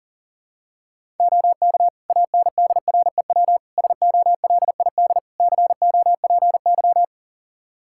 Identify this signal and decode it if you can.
Morse code, heard as a keyed tone, about 40 wpm, 700 Hz. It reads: OK ANDREW SOLID COPY
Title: Practice QSO - Exchange 3 - More Info - 40wpm